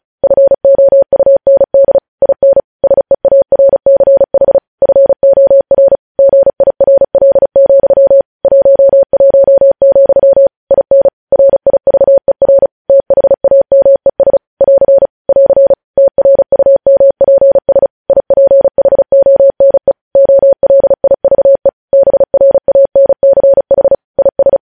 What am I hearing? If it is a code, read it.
Morse code, heard as a keyed tone, about 36 wpm, 550 Hz. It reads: FOUND IN SEARCH FOR GIRL, 11, IN RIVER THAMES <AR> <AR> TRUMPS IPHONE OLIVE BRANCH IS
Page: news headlines in morse code at 35 wpm